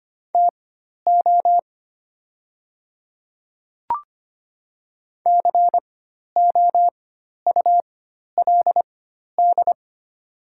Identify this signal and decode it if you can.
Morse code, heard as a keyed tone, about 25 wpm, spaced as for 12 wpm, 700 Hz. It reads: TO COULD
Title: 4x Character Spacing - Top 100 Words - 25wpm